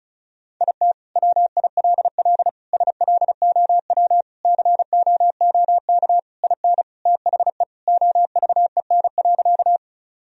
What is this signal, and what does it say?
Morse code, heard as a keyed tone, about 34 wpm, 700 Hz. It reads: IT WILL SLOW COOK IN THE OVEN.
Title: Sentences from Top 1000 Words - Review Part 4 - 35wpm